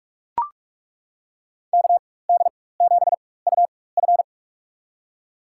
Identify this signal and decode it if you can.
Morse code, heard as a keyed tone, about 41 wpm, spaced as for 20 wpm, 700 Hz. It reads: KD7UF
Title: USA Call Signs - 4x Character Spacing - 45wpm